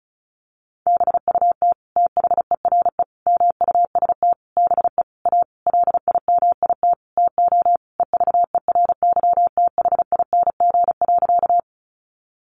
Morse code, heard as a keyed tone, about 34 wpm, 700 Hz. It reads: BUT THERE MUST BE A LIMIT TO EVERYTHING.
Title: Sentences from Top 1500 Words 35wpm